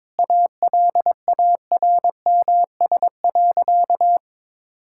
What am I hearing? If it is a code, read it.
Morse code, heard as a keyed tone, about 22 wpm, 700 Hz. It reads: ALARMS.
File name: practice / 22wpm / Easy Sentences with Hints - 22wpm